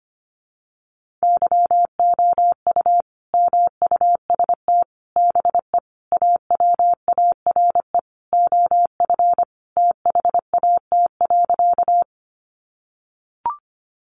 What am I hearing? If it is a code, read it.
Morse code, heard as a keyed tone, about 25 wpm, 700 Hz. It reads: YOU MUST BE AWARE OF THAT.
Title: Sentences from Top 1200 Words 25wpm